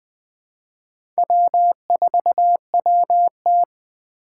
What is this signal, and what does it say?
Morse code, heard as a keyed tone, about 20 wpm, 700 Hz. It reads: W4WT
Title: USA Call Signs - 3x Character Spacing - 20wpm